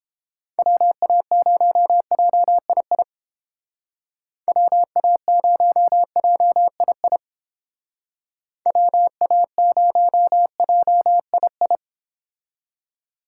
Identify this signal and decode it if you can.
Morse code, heard as a keyed tone, about 28 wpm, 700 Hz. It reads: WA0JII WA0JII WA0JII